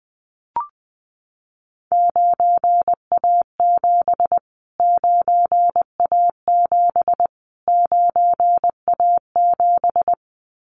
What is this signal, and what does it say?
Morse code, heard as a keyed tone, about 20 wpm, 700 Hz. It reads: 9A7 9A7 9A7
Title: Callsign Prefixes for Top 25 DXCC - Repeated 3x - 20wpm